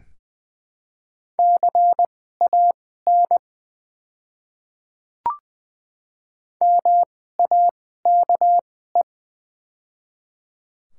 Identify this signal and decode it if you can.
Morse code, heard as a keyed tone, about 20 wpm, spaced as for 14 wpm, 700 Hz. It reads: CAN MAKE
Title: 2x Character Spacing - Top 100 Words - 20wpm